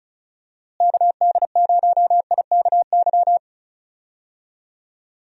Morse code, heard as a keyed tone, about 34 wpm, 700 Hz. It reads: KD0IKY
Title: USA Call Signs - 6 Characters - Repeated 3x - 35wpm